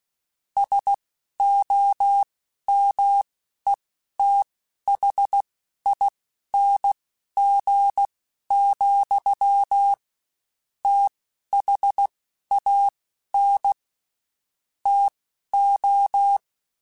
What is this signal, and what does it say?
Morse code, heard as a keyed tone, about 16 wpm, spaced as for 11 wpm, 800 Hz. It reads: SOMETHING, THAN TO
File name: Quotes Spaced / Quotes Spaced-mp3s / StandFor sp @15